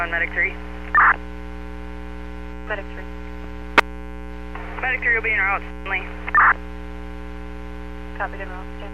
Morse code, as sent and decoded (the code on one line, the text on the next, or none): none